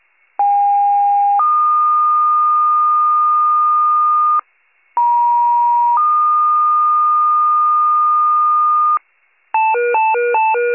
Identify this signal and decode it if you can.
Two-tone page, heard as a one-second tone, then a three-second tone, about 810 then 1220 Hz, about 950 then 1220 Hz. It is heard from 0.4 to 9.0 s.